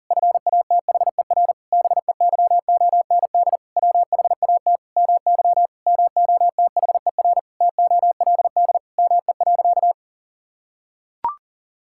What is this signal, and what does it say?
Morse code, heard as a keyed tone, about 40 wpm, 700 Hz. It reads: FATHER BEYOND WHAT MY MOTHER TOLD ME.
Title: Sentences from Top 1000 Words - Review Part 6 - 40wpm